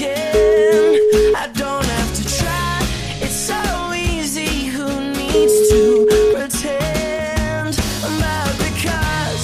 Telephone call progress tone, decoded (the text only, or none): ringback tone